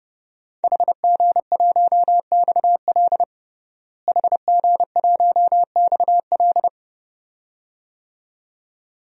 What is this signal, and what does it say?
Morse code, heard as a keyed tone, about 30 wpm, 700 Hz. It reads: HG1XL HG1XL
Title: DXCC-100 Call Signs - 5 Characters - Repeated 2x - 30wpm